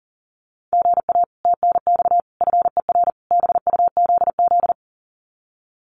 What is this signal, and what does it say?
Morse code, heard as a keyed tone, about 40 wpm, 700 Hz. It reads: GA TNX FER BUZZ
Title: Common QSO Phrases Encore 40wpm